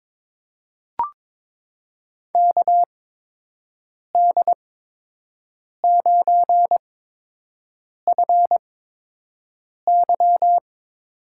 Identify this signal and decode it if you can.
Morse code, heard as a keyed tone, about 23 wpm, spaced as for 6 wpm, 700 Hz. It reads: KD9FY
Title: USA Call Signs - 6 Characters - 8x Character Spacing - 22wpm